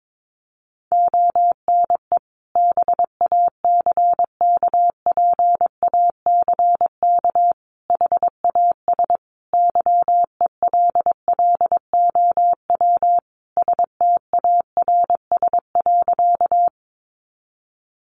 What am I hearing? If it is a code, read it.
Morse code, heard as a keyed tone, about 22 wpm, 700 Hz. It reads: ONE BACKPACK HAS YELLOW STARS.
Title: Additional Easy Phrases 22wpm